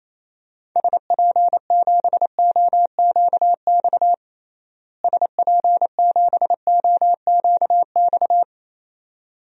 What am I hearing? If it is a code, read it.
Morse code, heard as a keyed tone, about 28 wpm, 700 Hz. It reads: SP7OQX SP7OQX
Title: DXCC-100 Call Signs - 6 Characters - Repeated 2x - 28wpm